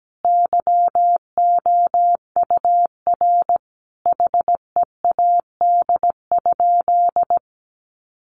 Morse code, heard as a keyed tone, about 17 wpm, 700 Hz. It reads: YOUR HEAD?
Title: Sentences from Top 600 Words 17wpm